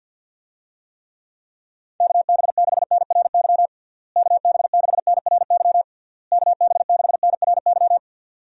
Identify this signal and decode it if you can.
Morse code, heard as a keyed tone, about 49 wpm, 700 Hz. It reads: KB6NRY KB6NRY KB6NRY
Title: USA Call Signs - 6 Characters - Repeated 3x - 50wpm